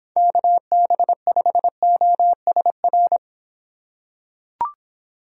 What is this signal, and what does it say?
Morse code, heard as a keyed tone, about 26 wpm, 700 Hz. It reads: KB5OSR